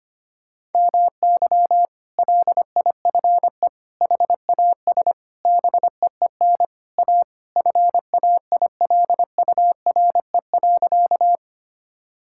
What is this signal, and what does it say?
Morse code, heard as a keyed tone, about 25 wpm, 700 Hz. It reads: MY LIFE HAS BEEN A FAILURE.
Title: Sentences from Top 1000 Words - Review Part 6 - 25wpm